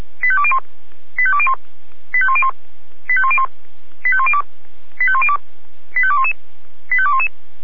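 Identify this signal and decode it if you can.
Five-tone selective call (ZVEI), heard as quick runs of five tones, tones of about 70 ms. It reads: 85201 85201 85101 85101 85102 85102 85211 85211
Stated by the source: ZVEI five-tone selective call 85264 received by SDR.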